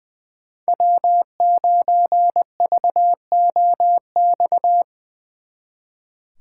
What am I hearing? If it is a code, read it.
Morse code, heard as a keyed tone, about 24 wpm, 700 Hz. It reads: W9VOX
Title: Speed Racing - USA 1x3 Call Signs 20wpm